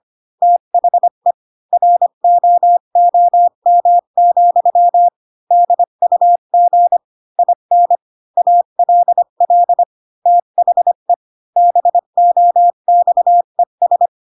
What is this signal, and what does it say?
Morse code, heard as a keyed tone, about 25 wpm, 700 Hz. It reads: THE ROOM, DUG IN ALL THE BOXES
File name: Pinocchio / Pinocchio ch5 20 25 ik1hge